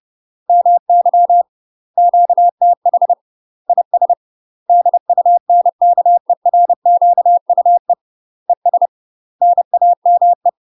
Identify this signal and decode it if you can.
Morse code, heard as a keyed tone, about 30 wpm, 700 Hz. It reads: MY QTH IS DUNKERQUE ES NAME
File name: UK MASTER QSOs TRAINING / 30WPM / UK MASTER QSOs 0003